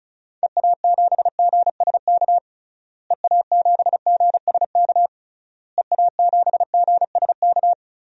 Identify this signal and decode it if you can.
Morse code, heard as a keyed tone, about 34 wpm, 700 Hz. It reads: EA7GSK EA7GSK EA7GSK
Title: DXCC-100 Call Signs - 6 Characters - Repeated 3x - 35wpm